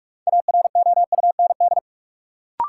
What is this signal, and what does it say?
Morse code, heard as a keyed tone, about 46 wpm, 700 Hz. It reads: AROUND
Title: Sentences from Top 600 Words - New Word Drill - 45wpm